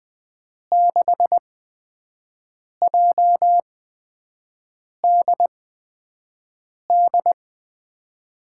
Morse code, heard as a keyed tone, about 20 wpm, spaced as for 5 wpm, 700 Hz. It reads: 6JDD
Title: USA Call Signs - 6 Characters - 8x Character Spacing - 20wpm